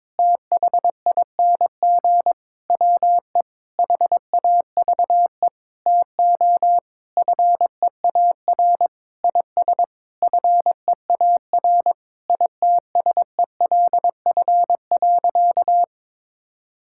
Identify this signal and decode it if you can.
Morse code, heard as a keyed tone, about 22 wpm, 700 Hz. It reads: THING WE HAVE TO FEAR IS FEAR ITSELF.
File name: practice / 22wpm / Sentences with 8-Word Phrases - 22wpm